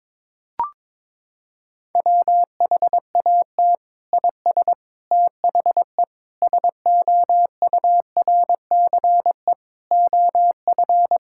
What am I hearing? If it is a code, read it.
Morse code, heard as a keyed tone, about 22 wpm, 700 Hz. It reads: WHAT IS THE SOURCE OF
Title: Sentences from Top 1000 Words - Review Part 5 - 22wpm